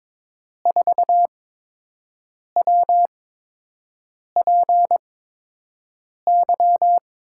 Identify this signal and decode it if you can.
Morse code, heard as a keyed tone, about 22 wpm, spaced as for 6 wpm, 700 Hz. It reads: 4WPY
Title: USA Call Signs - 8x Character Spacing - 22wpm